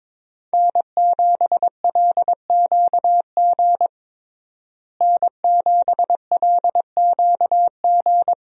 Morse code, heard as a keyed tone, about 22 wpm, 700 Hz. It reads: N7LQG N7LQG
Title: USA Call Signs - 5 Characters - Repeated 3x - 22wpm